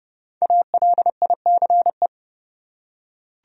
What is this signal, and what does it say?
Morse code, heard as a keyed tone, about 30 wpm, 700 Hz. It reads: ALICE
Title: USA Common Names - 30wpm